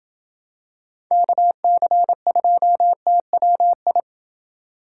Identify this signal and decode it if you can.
Morse code, heard as a keyed tone, about 27 wpm, 700 Hz. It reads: KC2TWI